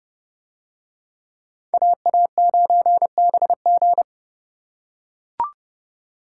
Morse code, heard as a keyed tone, about 29 wpm, 700 Hz. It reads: AA9BG